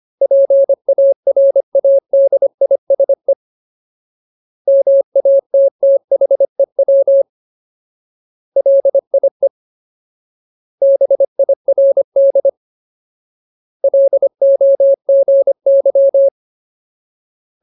Morse code, heard as a keyed tone, about 25 wpm, 550 Hz. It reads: PARADISE MATTHEW LIE BIRD LOGY
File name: Ultimate Word List / 25WPM Ultimate